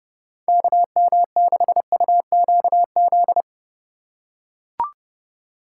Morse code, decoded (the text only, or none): KM6UQZ